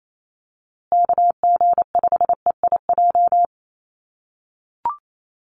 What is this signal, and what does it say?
Morse code, heard as a keyed tone, about 28 wpm, 700 Hz. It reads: KG5EIJ
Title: USA Call Signs Repeated 3x 28wpm